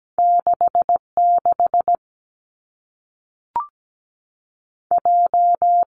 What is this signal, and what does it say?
Morse code, heard as a keyed tone, about 17 wpm, 700 Hz. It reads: 6 6 J
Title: Two Challenging Letter-Number Combinations 17wpm